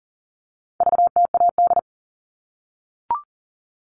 Morse code, heard as a keyed tone, about 40 wpm, 700 Hz. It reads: VTAD
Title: All Cut Numbers - 4-Digit 40wpm